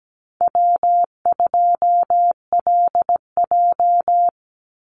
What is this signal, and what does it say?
Morse code, heard as a keyed tone, about 17 wpm, 700 Hz. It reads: W2LJ
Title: Easy Call Signs - 17wpm